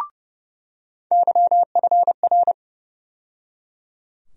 Morse code, heard as a keyed tone, about 30 wpm, 700 Hz. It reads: YFR
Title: USA Call Sign Postfixes - 3 Letters 30wpm